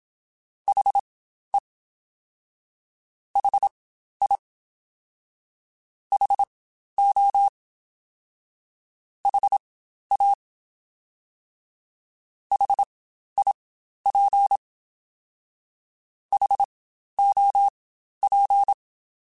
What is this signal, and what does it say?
Morse code, heard as a keyed tone, about 27 wpm, spaced as for 12 wpm, 800 Hz. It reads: HE HI HO HA HIP HOP
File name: Spaced Alphas